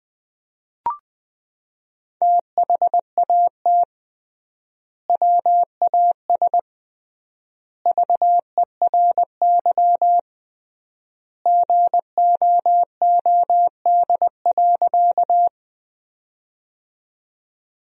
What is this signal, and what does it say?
Morse code, heard as a keyed tone, about 20 wpm, 700 Hz. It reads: THAT WAS VERY GOOD.
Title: 3x Word-Spacing - Sentences from Top 200 Words - 20wpm